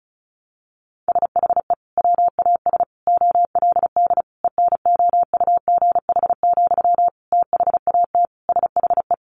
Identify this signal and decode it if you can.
Morse code, heard as a keyed tone, about 34 wpm, 700 Hz. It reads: SHE WAS OLD ENOUGH, THAT SHE
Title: Sentences from Top 800 Words 35wpm